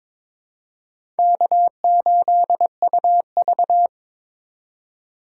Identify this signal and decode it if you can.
Morse code, heard as a keyed tone, about 22 wpm, 700 Hz. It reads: K8UV
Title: USA Call Signs - 4 Characters - 2x Character Spacing - 22wpm